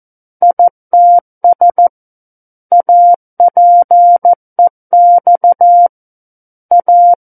Morse code, heard as a keyed tone, about 14 wpm, 700 Hz. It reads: ITS APEX A
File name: Book One / 05-Chapter 5 - 14 WPM